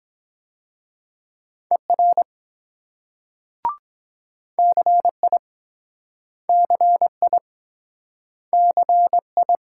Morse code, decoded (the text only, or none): ER CI CI CI